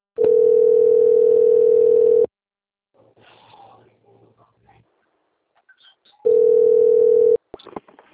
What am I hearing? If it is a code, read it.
Ringback tone (North America), 440 + 480 Hz.